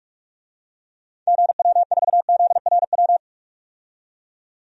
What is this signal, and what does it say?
Morse code, heard as a keyed tone, about 43 wpm, 700 Hz. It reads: GW4ZRW
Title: DXCC-100 Call Signs - 6 Characters - 45wpm